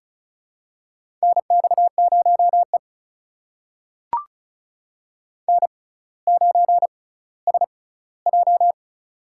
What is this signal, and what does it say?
Morse code, heard as a keyed tone, about 36 wpm, spaced as for 18 wpm, 700 Hz. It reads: NX0E N9SJ
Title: USA Call Signs - 6x Character Spacing - 35wpm